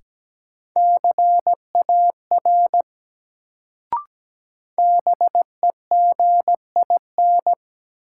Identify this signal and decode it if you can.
Morse code, heard as a keyed tone, about 17 wpm, 700 Hz. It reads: CAR BEGIN
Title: Sentences from Top 300 Words - New Word Drill - 17wpm